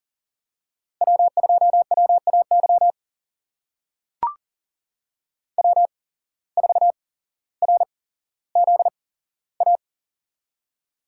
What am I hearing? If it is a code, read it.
Morse code, heard as a keyed tone, about 38 wpm, spaced as for 17 wpm, 700 Hz. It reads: W2WAY W4RZA
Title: USA Call Signs - 5 Characters - 8x Character Spacing - 40wpm